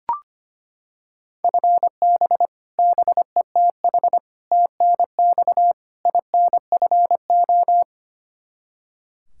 Morse code, heard as a keyed tone, about 25 wpm, 700 Hz. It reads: FB BETH TNX INFO
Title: Practice QSO - Exchange 3 - More Info - 25wpm